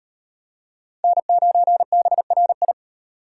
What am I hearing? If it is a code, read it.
Morse code, heard as a keyed tone, about 38 wpm, 700 Hz. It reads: N9BRI